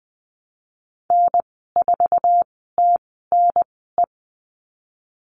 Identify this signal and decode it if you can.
Morse code, heard as a keyed tone, about 20 wpm, spaced as for 15 wpm, 700 Hz. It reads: N4TNE
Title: 069 Lesson 12g - Letter d - Call Signs 20wpm